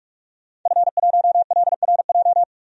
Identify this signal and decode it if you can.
Morse code, heard as a keyed tone, about 46 wpm, 700 Hz. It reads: F1LRJ
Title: DXCC-100 Call Signs - 5 Characters - 45wpm